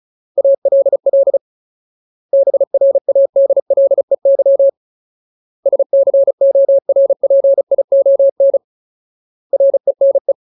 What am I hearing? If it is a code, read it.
Morse code, heard as a keyed tone, about 34 wpm, 550 Hz. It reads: ALL BRADLEY SCORPION RENE
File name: Ultimate Word List / 35WPM Ultimate